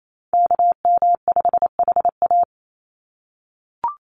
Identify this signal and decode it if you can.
Morse code, heard as a keyed tone, about 28 wpm, 700 Hz. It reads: KM5HA
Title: USA Call Signs 28wpm